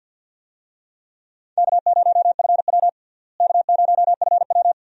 Morse code, heard as a keyed tone, about 51 wpm, 700 Hz. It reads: K0FW K0FW
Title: USA Call Signs - 4 Characters - Repeated 2x - 50wpm